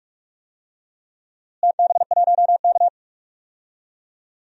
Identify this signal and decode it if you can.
Morse code, heard as a keyed tone, about 45 wpm, 700 Hz. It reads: TB1K
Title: DXCC-100 Call Signs - 4 Characters - Repeated 2x - 45wpm